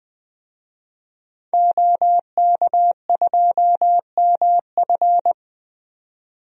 Morse code, heard as a keyed tone, about 20 wpm, 700 Hz. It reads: OK2MF
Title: DXCC-100 Call Signs - 5 Characters - 20wpm